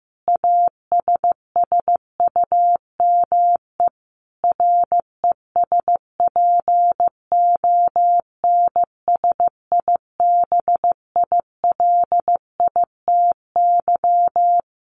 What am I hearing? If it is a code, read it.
Morse code, heard as a keyed tone, about 15 wpm, 700 Hz. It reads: ASSUME RESPONSIBILITY
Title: Sentences from Top 1500 Words 15wpm